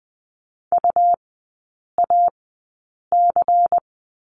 Morse code, heard as a keyed tone, about 20 wpm, spaced as for 8 wpm, 700 Hz. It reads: UAC
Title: 100 Lesson 17e - Letter u - Three Characters 20wpm